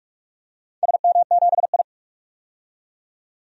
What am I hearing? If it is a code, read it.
Morse code, heard as a keyed tone, about 48 wpm, 700 Hz. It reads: SM7I
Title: DXCC-100 Call Signs - 4 Characters - 45wpm